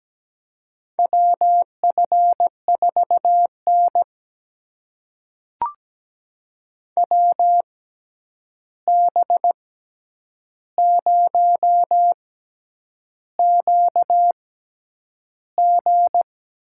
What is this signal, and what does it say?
Morse code, heard as a keyed tone, about 17 wpm, spaced as for 8 wpm, 700 Hz. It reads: WF4N WB0QG
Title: USA Call Signs - 6x Character Spacing - 17wpm